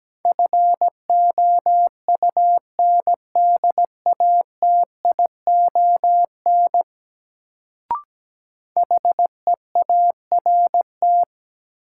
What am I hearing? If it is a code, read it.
Morse code, heard as a keyed tone, about 17 wpm, 700 Hz. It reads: FOUNDATION HEART
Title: Sentences from Top 1200 Words - New Word Drill - 17wpm